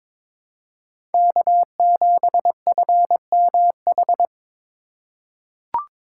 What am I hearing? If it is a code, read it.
Morse code, heard as a keyed tone, about 22 wpm, 700 Hz. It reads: K7FMH
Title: USA Call Signs - 2x Character Spacing - 22wpm